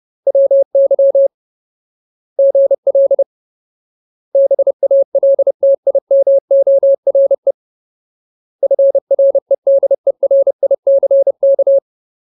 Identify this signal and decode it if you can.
Morse code, heard as a keyed tone, about 30 wpm, 550 Hz. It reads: WY GL BALTIMORE FREDERICK
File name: Ultimate Word List / 30WPM Ultimate